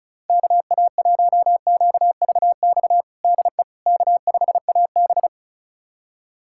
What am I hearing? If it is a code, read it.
Morse code, heard as a keyed tone, about 36 wpm, 700 Hz. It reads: KA1QVX DE K5AB
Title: USA Callsign DE Callsign 35wpm